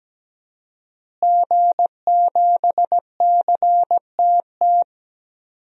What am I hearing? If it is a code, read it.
Morse code, heard as a keyed tone, about 17 wpm, 700 Hz. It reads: G7CTT
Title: DXCC-100 Call Signs - 5 Characters - Repeated 3x - 17wpm